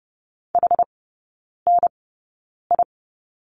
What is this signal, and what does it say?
Morse code, heard as a keyed tone, about 30 wpm, spaced as for 9 wpm, 700 Hz. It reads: HNI
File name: 081 Lesson 14e - Number 2 - Three Characters 30wpm